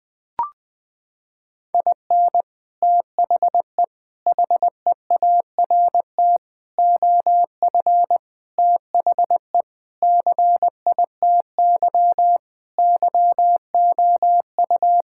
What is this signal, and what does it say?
Morse code, heard as a keyed tone, about 20 wpm, 700 Hz. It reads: IN THE HEART OF THE CITY YOU
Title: Sentences with 3-Word Phrases - 20wpm